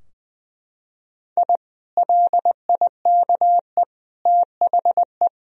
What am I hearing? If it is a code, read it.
Morse code, heard as a keyed tone, about 20 wpm, 700 Hz. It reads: I LIKE THE
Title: Sentences from Top 1000 Words - Review Part 2 - 20wpm